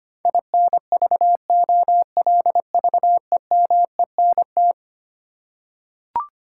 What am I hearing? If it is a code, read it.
Morse code, heard as a keyed tone, about 25 wpm, 700 Hz. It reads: INVOLVEMENT
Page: morse-code-ninja-practice-25wpm